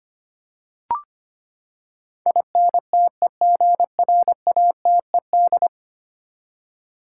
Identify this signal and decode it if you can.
Morse code, heard as a keyed tone, about 25 wpm, 700 Hz. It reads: INTEGRATED
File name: Ten Letter Words - 25wpm